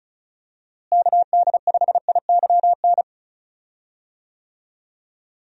Morse code, heard as a keyed tone, about 36 wpm, 700 Hz. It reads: KD5IYN
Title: USA Call Signs - 6 Characters - Repeated 3x - 35wpm